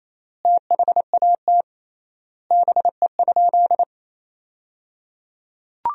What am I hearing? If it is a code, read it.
Morse code, heard as a keyed tone, about 28 wpm, 700 Hz. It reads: THAT BE?
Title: 3x Word-Spacing - Sentences from Top 100 Words - 28wpm